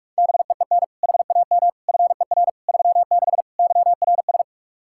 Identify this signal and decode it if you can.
Morse code, heard as a keyed tone, about 46 wpm, 700 Hz. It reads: BEEN HAM FER 36 YRS
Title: Practice QSO - Exchange 3 - More Info - 45wpm